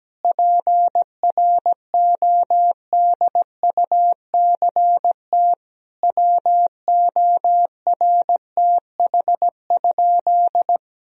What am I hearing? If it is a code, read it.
Morse code, heard as a keyed tone, about 17 wpm, 700 Hz. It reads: PRODUCT WORTH?
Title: Sentences from Top 1000 Words - Review Part 2 - 17wpm